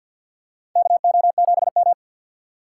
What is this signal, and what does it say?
Morse code, heard as a keyed tone, about 49 wpm, 700 Hz. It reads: KO7M
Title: USA Call Signs - 4 Characters - 6x Character Spacing - 50wpm